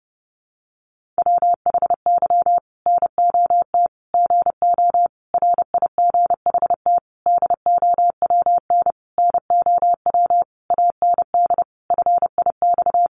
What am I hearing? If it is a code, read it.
Morse code, heard as a keyed tone, about 30 wpm, 700 Hz. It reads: WHY NOT GO RIGHT DOWN NOW AND FIX
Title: Sentences from Top 1400 Words 30wpm